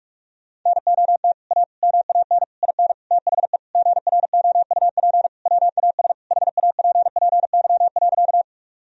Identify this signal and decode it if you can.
Morse code, heard as a keyed tone, about 46 wpm, 700 Hz. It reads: NOT A MAN IN THE GROUP WAS HAPPY.
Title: Sentences from Top 1000 Words - Review Part 1 - 45wpm